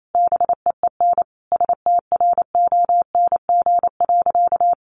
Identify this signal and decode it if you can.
Morse code, heard as a keyed tone, about 28 wpm, 700 Hz. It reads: BEEN STRONG.
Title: Sentences from Top 1400 Words 28wpm